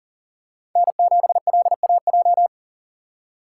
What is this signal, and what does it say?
Morse code, heard as a keyed tone, about 40 wpm, 700 Hz. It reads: N7LAJ
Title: USA Call Signs - 5 Characters - 2x Character Spacing - 40wpm